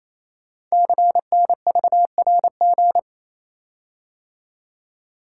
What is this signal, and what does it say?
Morse code, heard as a keyed tone, about 27 wpm, 700 Hz. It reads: CNVRG
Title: Code Groups - 5 Letters - 28wpm